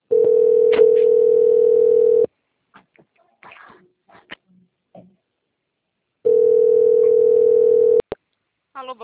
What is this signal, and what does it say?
Ringback tone (North America), 440 + 480 Hz.
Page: Cornelio Procopio 2